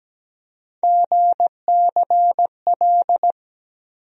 Morse code, heard as a keyed tone, about 17 wpm, 700 Hz. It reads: GCL
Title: USA Call Sign Postfixes - 3 Letters 17wpm